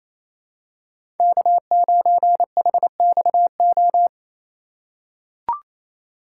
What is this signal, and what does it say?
Morse code, heard as a keyed tone, about 27 wpm, 700 Hz. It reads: K9HXO
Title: USA Call Signs - 5 Characters - 2x Character Spacing - 28wpm